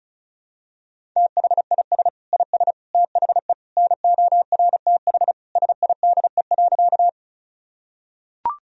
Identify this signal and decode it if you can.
Morse code, heard as a keyed tone, about 34 wpm, 700 Hz. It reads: THIS IS THE NORTH SIDE.
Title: Sentences from Top 1000 Words - Review Part 1 - 35wpm